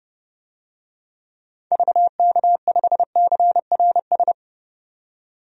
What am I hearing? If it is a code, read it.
Morse code, heard as a keyed tone, about 30 wpm, 700 Hz. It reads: VK5CRS
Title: DXCC-100 Call Signs - 6 Characters - 30wpm